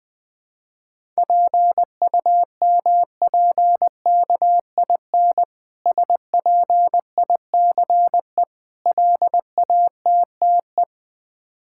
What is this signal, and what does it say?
Morse code, heard as a keyed tone, about 20 wpm, 690 Hz. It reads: PUMPKIN SPICE LATTE
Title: Difficult Words with Hints - 20wpm